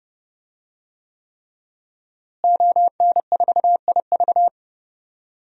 Morse code, heard as a keyed tone, about 32 wpm, 700 Hz. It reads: ON4IV
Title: DXCC-100 Call Signs - 5 Characters - Repeated 3x - 30wpm